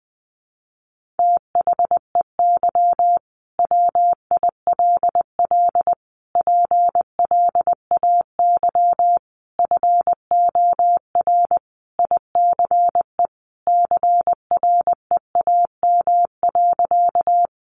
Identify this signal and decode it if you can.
Morse code, heard as a keyed tone, about 20 wpm, 700 Hz. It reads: THEY WILL PLAY FOR ICE CREAM.
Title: Sentences from Top 1500 Words 20wpm